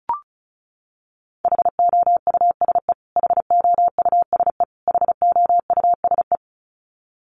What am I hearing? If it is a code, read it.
Morse code, heard as a keyed tone, about 34 wpm, 700 Hz. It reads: HOUSE HOUSE HOUSE
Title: Top 500 Words Repeated 35wpm